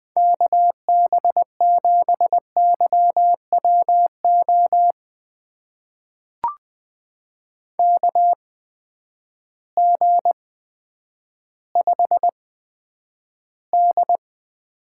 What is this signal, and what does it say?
Morse code, heard as a keyed tone, about 20 wpm, spaced as for 10 wpm, 700 Hz. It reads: KB7YWO KG5D
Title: USA Call Signs - 6 Characters - 8x Character Spacing - 20wpm